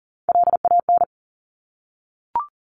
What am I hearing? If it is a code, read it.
Morse code, heard as a keyed tone, about 41 wpm, 700 Hz. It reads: LAN
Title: Top Three Letter Combinations in Words 40wpm